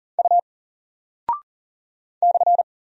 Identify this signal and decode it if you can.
Morse code, heard as a keyed tone, about 40 wpm, 700 Hz. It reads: U /
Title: Single Character (Letter:Number:Punctuation:Prosign) Alt 40wpm